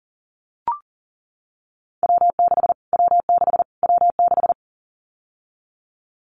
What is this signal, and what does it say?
Morse code, heard as a keyed tone, about 40 wpm, 700 Hz. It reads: W6 W6 W6
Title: Callsign Prefixes for Top 25 DXCC - Repeated 3x - 40wpm